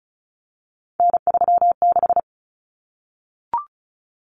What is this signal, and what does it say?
Morse code, heard as a keyed tone, about 34 wpm, 700 Hz. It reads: N36
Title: Common Cut Numbers - 3-Digit 35wpm